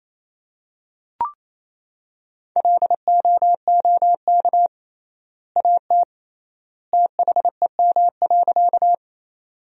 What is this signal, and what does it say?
Morse code, heard as a keyed tone, about 28 wpm, 700 Hz. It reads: LOOK AT THEM.
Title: 3x Word-Spacing - Sentences from Top 100 Words - 28wpm